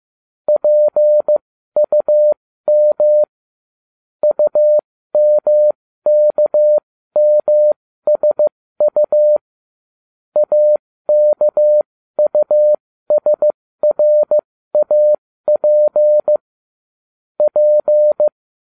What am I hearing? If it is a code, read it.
Morse code, heard as a keyed tone, about 15 wpm, spaced as for 13 wpm, 600 Hz. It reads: PUM UMKMSU AKUSRAP P